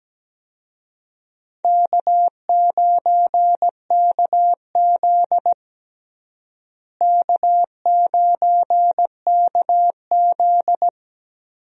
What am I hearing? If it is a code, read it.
Morse code, heard as a keyed tone, about 17 wpm, 700 Hz. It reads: K9KZ K9KZ
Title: USA Call Signs - 4 Characters - Repeated 3x - 17wpm